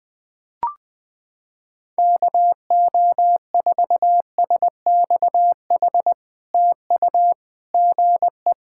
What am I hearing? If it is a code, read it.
Morse code, heard as a keyed tone, about 20 wpm, 700 Hz. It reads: KO4SXH TU GE
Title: POTA QSO Practice - 20wpm